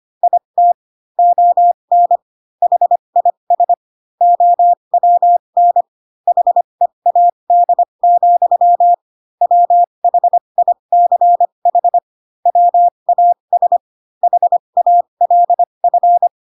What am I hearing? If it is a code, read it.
Morse code, heard as a keyed tone, about 25 wpm, spaced as for 22 wpm, 700 Hz. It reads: IT ON HIS OWN HEAD, WHICH WAS HALF
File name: Pinocchio / Pinocchio ch3 18 25 ik1hge